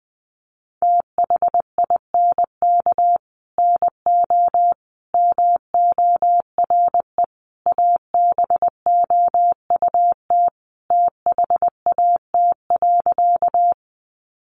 Morse code, heard as a keyed tone, about 20 wpm, 700 Hz. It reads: THINK NO MORE ABOUT THAT.
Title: Sentences from Top 200 Words 20wpm